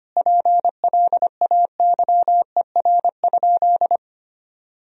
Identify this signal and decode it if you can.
Morse code, heard as a keyed tone, about 25 wpm, 700 Hz. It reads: PLAYER?
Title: Sentences from Top 1000 Words - Review Part 4 - 25wpm